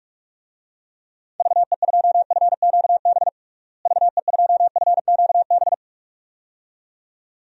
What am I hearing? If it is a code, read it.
Morse code, heard as a keyed tone, about 42 wpm, 700 Hz. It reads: VE2FQB VE2FQB
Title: DXCC-100 Call Signs - 6 Characters - Repeated 2x - 45wpm